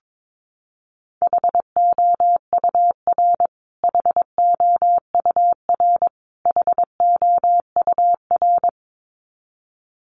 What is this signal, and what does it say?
Morse code, heard as a keyed tone, about 22 wpm, 700 Hz. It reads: HOUR HOUR HOUR
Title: Top 500 Words Repeated 22wpm